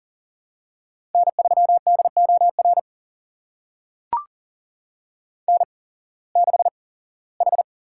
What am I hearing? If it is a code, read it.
Morse code, heard as a keyed tone, about 39 wpm, spaced as for 21 wpm, 700 Hz. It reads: N3DOR N6H
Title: USA Call Signs - 5 Characters - 8x Character Spacing - 40wpm